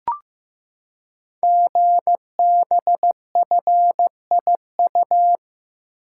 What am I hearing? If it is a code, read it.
Morse code, heard as a keyed tone, about 15 wpm, 700 Hz. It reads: GBFIU
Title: Code Groups - 5 Letters - 15wpm